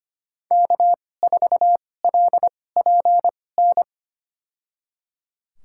Morse code, heard as a keyed tone, about 25 wpm, spaced as for 18 wpm, 700 Hz. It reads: K4LPN
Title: USA Call Signs - 5 Characters - 2x Character Spacing - 25wpm